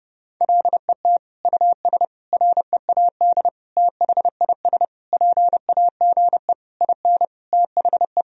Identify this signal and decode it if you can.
Morse code, heard as a keyed tone, about 30 wpm, 700 Hz. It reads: LET US READ THIS PAGE IN THE
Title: Sentences from Top 1000 Words - Review Part 1 - 30wpm